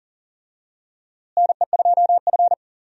Morse code, heard as a keyed tone, about 40 wpm, 700 Hz. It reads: NE2F
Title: USA Call Signs - 4 Characters - Repeated 3x - 40wpm